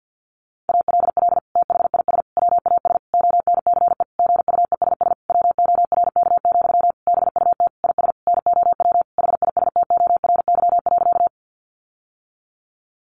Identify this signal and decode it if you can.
Morse code, heard as a keyed tone, about 50 wpm, 700 Hz. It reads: ALL THIS WAS ONCE GUESS WORK, BUT IS NOW HISTORY.
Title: Sentences from Top 1500 Words 50wpm